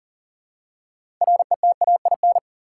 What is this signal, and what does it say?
Morse code, heard as a keyed tone, about 40 wpm, 700 Hz. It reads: RETAIN
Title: Six Letter Words - 40wpm